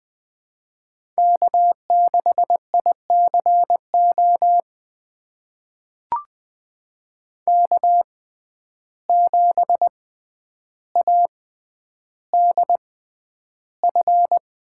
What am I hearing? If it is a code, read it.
Morse code, heard as a keyed tone, about 20 wpm, spaced as for 10 wpm, 700 Hz. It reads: K6ICO K7ADF
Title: USA Call Signs - 5 Characters - 6x Character Spacing - 20wpm